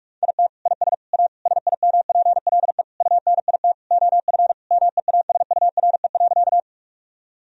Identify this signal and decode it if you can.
Morse code, heard as a keyed tone, about 46 wpm, 700 Hz. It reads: IT IS A SIMPLE UNIT OF MEASURE.
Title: Sentences from Top 1000 Words - Review Part 2 - 45wpm